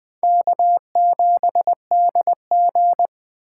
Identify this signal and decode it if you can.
Morse code, heard as a keyed tone, about 20 wpm, 700 Hz. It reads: K7DG
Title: USA Call Signs - 4 Characters - 8x Character Spacing - 20wpm